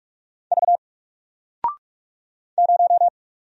Morse code, heard as a keyed tone, about 43 wpm, 700 Hz. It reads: V 0
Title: Single Character (Letter:Number:Punctuation:Prosign) Alt 45wpm